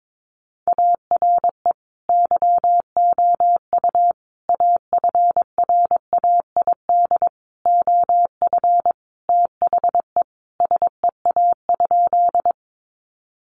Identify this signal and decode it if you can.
Morse code, heard as a keyed tone, about 22 wpm, 700 Hz. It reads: ARE YOU AFRAID OF THE SEA?
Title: Sentences from Top 1500 Words 22wpm